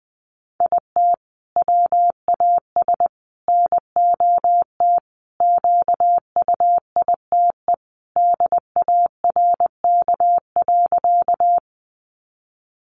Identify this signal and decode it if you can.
Morse code, heard as a keyed tone, about 20 wpm, 700 Hz. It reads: IT WAS NOT QUITE DARK.
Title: Sentences from Top 700 Words 20wpm